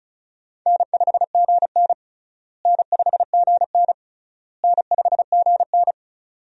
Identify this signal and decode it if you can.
Morse code, heard as a keyed tone, about 36 wpm, 700 Hz. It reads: N5GN N5GN N5GN
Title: USA Call Signs - 4 Characters - Repeated 3x - 35wpm